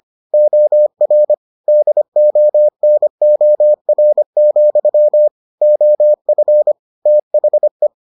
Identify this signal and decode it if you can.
Morse code, heard as a keyed tone, about 25 wpm, 600 Hz. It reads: OR DONOR, OF THE